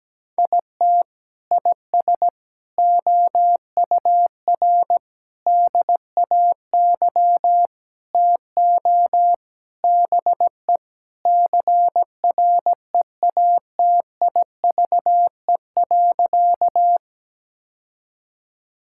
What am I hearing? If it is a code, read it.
Morse code, heard as a keyed tone, about 17 wpm, 700 Hz. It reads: IT IS OUR DAY TO BE CREATIVE.
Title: Sentences from Top 1000 Words - Review Part 5 - 17wpm